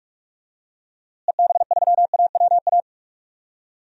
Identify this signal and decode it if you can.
Morse code, heard as a keyed tone, about 44 wpm, 700 Hz. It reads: EB3AWA
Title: DXCC-100 Call Signs - 6 Characters - Repeated 2x - 45wpm